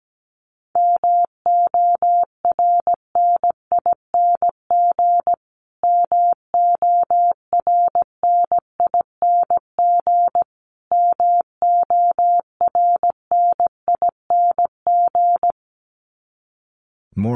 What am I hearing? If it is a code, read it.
Morse code, heard as a keyed tone, about 17 wpm, 700 Hz. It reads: MORNING MORNING MORNING
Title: Top 500 Words Repeated 17wpm